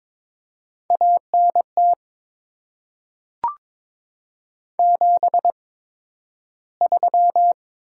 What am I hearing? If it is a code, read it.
Morse code, heard as a keyed tone, about 22 wpm, spaced as for 12 wpm, 700 Hz. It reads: ANT 73
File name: Top 20 QSO Elements - 8x Character Spacing - 22wpm